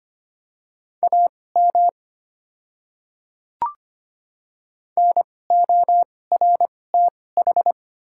Morse code, heard as a keyed tone, about 25 wpm, spaced as for 18 wpm, 700 Hz. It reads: AM NORTH
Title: 2x Character Spacing - Top 500 Words - 25wpm